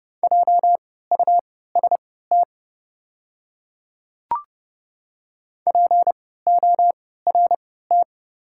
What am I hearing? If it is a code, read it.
Morse code, heard as a keyed tone, about 31 wpm, spaced as for 17 wpm, 690 Hz. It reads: JUST PORT
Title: 3x Character Spacing - Top 200 Words - 30wpm